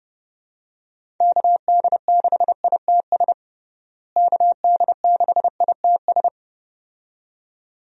Morse code, heard as a keyed tone, about 30 wpm, 700 Hz. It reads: KD6ITS KD6ITS
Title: USA Call Signs - 6 Characters - Repeated 2x - 30wpm